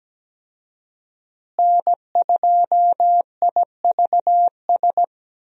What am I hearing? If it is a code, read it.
Morse code, heard as a keyed tone, about 17 wpm, 700 Hz. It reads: N2IVS
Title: USA Call Signs - 5 Characters - Repeated 2x - 17wpm